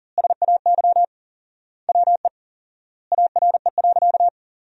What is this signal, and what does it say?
Morse code, heard as a keyed tone, about 40 wpm, 700 Hz. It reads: SAY WE ARE.
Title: 4x Word-Spacing - Sentences from Top 200 Words - 40wpm